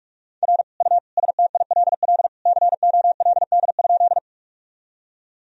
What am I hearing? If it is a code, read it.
Morse code, heard as a keyed tone, about 46 wpm, 700 Hz. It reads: R U STILL COLD?
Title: Sentences from Top 400 Words and Abbreviations 45wpm